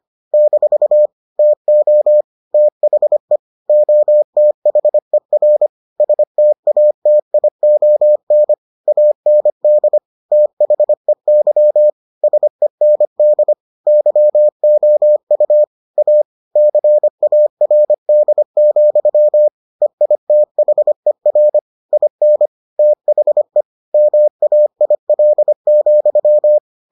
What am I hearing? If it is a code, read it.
Morse code, heard as a keyed tone, about 25 wpm, 600 Hz. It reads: - TO THE OTHER STATION AND THEY SEND YOU A CARD, EITHER IN THE MAIL,